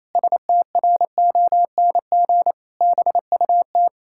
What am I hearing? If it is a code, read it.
Morse code, heard as a keyed tone, about 28 wpm, 700 Hz. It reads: STRONG BUT
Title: Pangram Challenge v2 - 28wpm